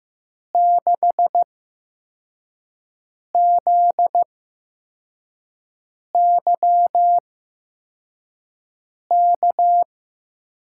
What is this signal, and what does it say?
Morse code, heard as a keyed tone, about 15 wpm, spaced as for 4 wpm, 700 Hz. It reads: 6ZYK
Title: USA Call Signs - 6 Characters - 8x Character Spacing - 15wpm